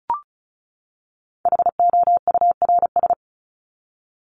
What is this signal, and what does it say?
Morse code, heard as a keyed tone, about 34 wpm, 700 Hz. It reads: HOURS
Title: ES:S Suffix - Top 100 Words with Base Words 35wpm